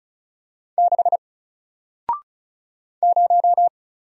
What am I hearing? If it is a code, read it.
Morse code, heard as a keyed tone, about 36 wpm, 700 Hz. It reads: 6 0
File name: Single-Letter-Number-35wpm